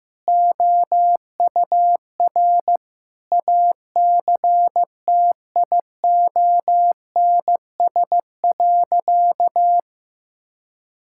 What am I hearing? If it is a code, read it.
Morse code, heard as a keyed tone, about 15 wpm, 700 Hz. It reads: OUR ACTIONS.
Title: Sentences from Top 1000 Words - Review Part 6 - 15wpm